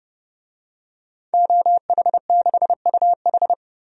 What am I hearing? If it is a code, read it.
Morse code, heard as a keyed tone, about 30 wpm, 700 Hz. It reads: OH6UH
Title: DXCC-100 Call Signs - 5 Characters - Repeated 2x - 30wpm